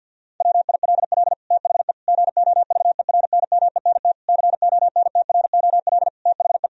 Morse code, heard as a keyed tone, about 52 wpm, 700 Hz. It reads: WILL THE GOVERNMENT CONTROL THE
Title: Sentences from Top 1000 Words - Review Part 3 - 50wpm